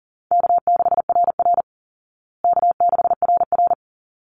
Morse code, heard as a keyed tone, about 40 wpm, 700 Hz. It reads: K6RR K6RR
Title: Easy Call Signs Repeated 2x - 40wpm